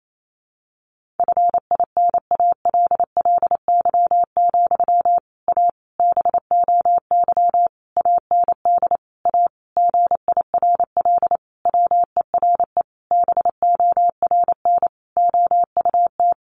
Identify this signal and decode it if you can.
Morse code, heard as a keyed tone, about 28 wpm, 700 Hz. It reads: FINALLY, A BOY AND A GIRL WERE BORN OUT
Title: Sentences from Top 1200 Words 28wpm